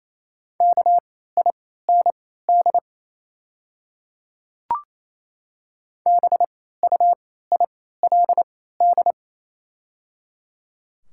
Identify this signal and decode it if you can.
Morse code, heard as a keyed tone, about 28 wpm, spaced as for 16 wpm, 700 Hz. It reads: KIND BUILD
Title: 3x Character Spacing - Top 200 Words - 28wpm